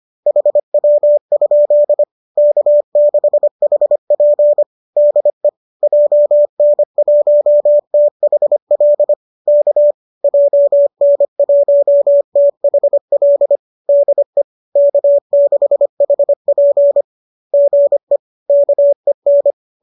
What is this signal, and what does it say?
Morse code, heard as a keyed tone, about 25 wpm, 590 Hz. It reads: HW? K6HP DE JN1THL K JN1THL DE K6HP GE KEN
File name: qso205 25